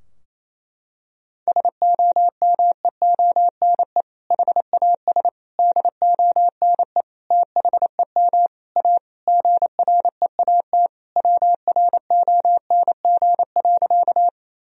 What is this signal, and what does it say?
Morse code, heard as a keyed tone, about 27 wpm, 700 Hz. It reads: SOMEONE HAS DONE THEM A GREAT WRONG.
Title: Sentences from Top 1000 Words - Review Part 4 - 28wpm